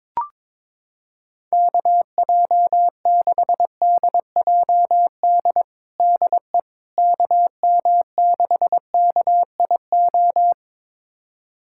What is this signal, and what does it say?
Morse code, heard as a keyed tone, about 22 wpm, 700 Hz. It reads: KJ6DJD DE KM6KIO
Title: USA Callsign DE Callsign 22wpm